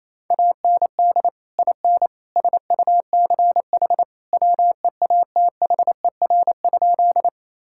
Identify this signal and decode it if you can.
Morse code, heard as a keyed tone, about 27 wpm, 700 Hz. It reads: AND IN SUCH WEATHER?
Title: Sentences from Top 1000 Words - Review Part 5 - 28wpm